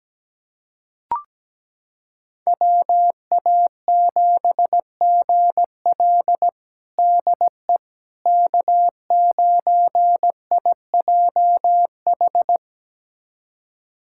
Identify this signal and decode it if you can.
Morse code, heard as a keyed tone, about 17 wpm, 700 Hz. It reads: WA7GL DE K9IJH
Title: USA Callsign DE Callsign - 5 Characters - 17wpm